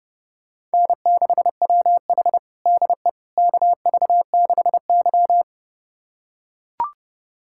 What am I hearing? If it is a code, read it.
Morse code, heard as a keyed tone, about 29 wpm, 700 Hz. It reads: N6WH DE KV6Y
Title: USA Callsign DE Callsign - 4 Characters - 30wpm